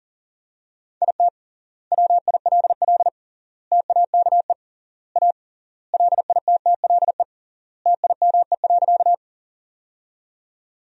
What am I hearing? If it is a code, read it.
Morse code, heard as a keyed tone, about 40 wpm, 700 Hz. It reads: IT WILL TAKE A LITTLE TIME.
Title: 3x Word-Spacing - Sentences from Top 200 Words - 40wpm